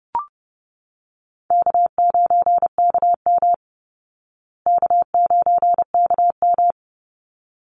Morse code, heard as a keyed tone, about 30 wpm, 700 Hz. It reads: K9KM K9KM
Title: Call Signs Repeated 3x - 30wpm